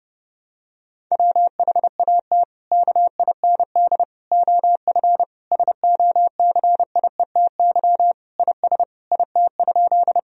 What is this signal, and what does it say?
Morse code, heard as a keyed tone, about 30 wpm, 700 Hz. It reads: WHAT KIND OF SOCIETY IS IT?
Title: Sentences from Top 1000 Words - Review Part 4 - 30wpm